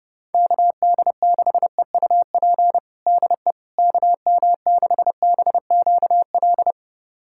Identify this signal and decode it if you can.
Morse code, heard as a keyed tone, about 30 wpm, 700 Hz. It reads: KD6EUP DE KM6BQL
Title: USA Callsign DE Callsign - 6 Characters - 30wpm